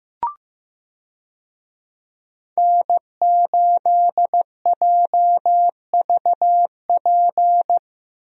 Morse code, heard as a keyed tone, about 15 wpm, 700 Hz. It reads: N8JVP
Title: USA Call Signs - 5 Characters - Repeated 2x - 15wpm